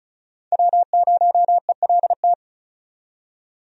Morse code, heard as a keyed tone, about 34 wpm, 700 Hz. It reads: W0ELT
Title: USA Call Signs - 5 Characters - 35wpm